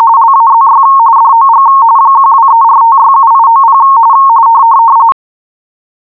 RTTY, about 45.45 baud. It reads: LAZY DOG 1234567890